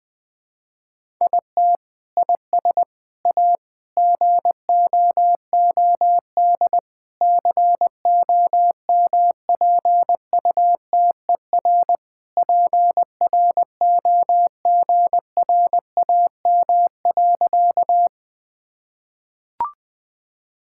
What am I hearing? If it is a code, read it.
Morse code, heard as a keyed tone, about 20 wpm, 700 Hz. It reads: IT IS A GOOD COMPUTER PROGRAM.
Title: Sentences from Top 1000 Words - Review Part 5 - 20wpm